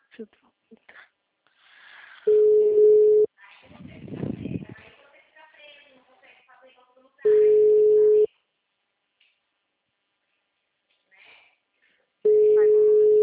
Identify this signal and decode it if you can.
Ringback tone (Europe, 425 Hz), with speech.